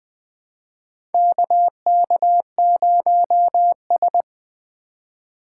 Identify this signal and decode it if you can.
Morse code, heard as a keyed tone, about 20 wpm, 700 Hz. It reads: KK0S
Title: USA Call Signs - 4 Characters - 6x Character Spacing - 20wpm